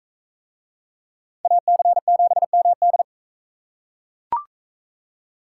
Morse code, heard as a keyed tone, about 42 wpm, 700 Hz. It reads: AC7MD